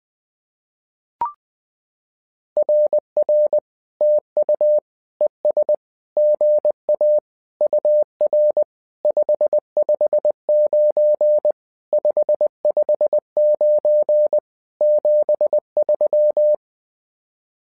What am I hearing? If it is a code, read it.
Morse code, heard as a keyed tone, about 20 wpm, 600 Hz. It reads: RR TU ES GA UR 559 559 73
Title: SOTA QSO Practice 20wpm